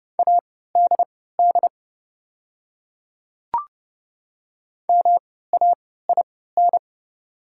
Morse code, heard as a keyed tone, about 30 wpm, spaced as for 17 wpm, 700 Hz. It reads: ADD MAIN
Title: 3x Character Spacing - Top 300 Words - 30wpm